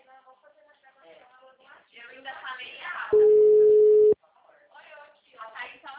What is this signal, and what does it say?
Ringback tone (Europe, 425 Hz), with speech.